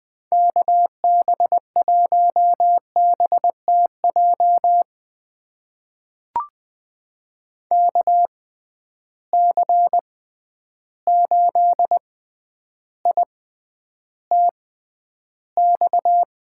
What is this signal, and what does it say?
Morse code, heard as a keyed tone, about 20 wpm, spaced as for 10 wpm, 700 Hz. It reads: KB1BTJ KC8ITX
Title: USA Call Signs - 6 Characters - 6x Character Spacing - 20wpm